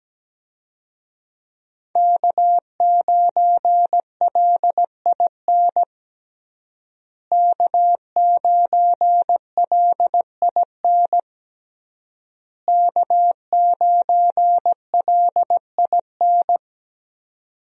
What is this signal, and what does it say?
Morse code, heard as a keyed tone, about 17 wpm, 700 Hz. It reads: K9LIN K9LIN K9LIN
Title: USA Call Signs - 5 Characters - Repeated 3x - 17wpm